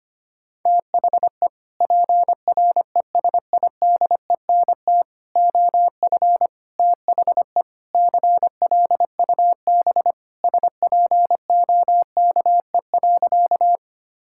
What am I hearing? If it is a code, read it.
Morse code, heard as a keyed tone, about 25 wpm, 700 Hz. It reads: THE PRESIDENT OF THE CLUB SPOKE.
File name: practice / 25wpm / Sentences with 4-Word Phrases - 25wpm